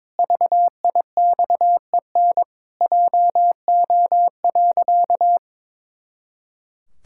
Morse code, heard as a keyed tone, about 22 wpm, 700 Hz. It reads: VIXEN JO.
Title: Pangram Challenge v2 - 22wpm